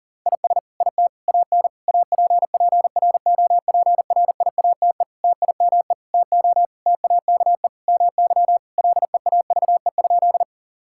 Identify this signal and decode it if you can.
Morse code, heard as a keyed tone, about 40 wpm, 700 Hz. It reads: IS IT AN APPROPRIATE TIME TO TAKE MY LEAVE?
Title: Sentences from Top 1000 Words - Review Part 5 - 40wpm